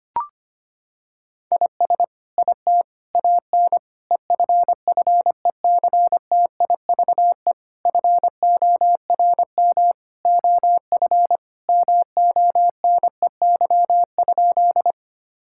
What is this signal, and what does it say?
Morse code, heard as a keyed tone, about 25 wpm, 700 Hz. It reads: IS IT AN EFFECTIVE FORM OF MONEY?
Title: Sentences from Top 1000 Words - Review Part 6 - 25wpm